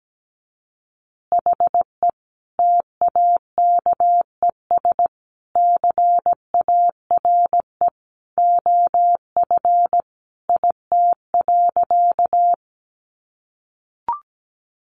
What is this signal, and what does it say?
Morse code, heard as a keyed tone, about 17 wpm, 700 Hz. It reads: HE TAKES CARE OF IT.
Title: Additional Easy Phrases 17wpm